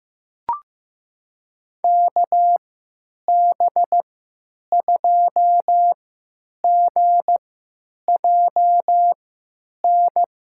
Morse code, heard as a keyed tone, about 15 wpm, spaced as for 9 wpm, 700 Hz. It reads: KB2GJN
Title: USA Call Signs - 3x Character Spacing - 15wpm